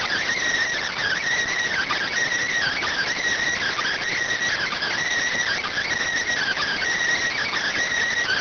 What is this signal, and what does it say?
Emergency Alert System (SAME) tones (header bursts), heard as harsh buzzing bursts. It is heard from 0.6 to 7.5 s.